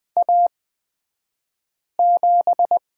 Morse code, heard as a keyed tone, about 20 wpm, spaced as for 5 wpm, 700 Hz. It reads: A7
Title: 223 Lesson 36d - BT - Two Characters 20wpm